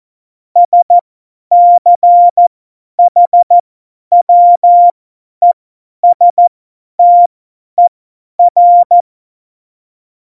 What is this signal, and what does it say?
Morse code, heard as a keyed tone, about 14 wpm, spaced as for 10 wpm, 700 Hz. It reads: SCHWESTER
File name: Morsekurs / Witze / Witz 18. Dezember, 50 F70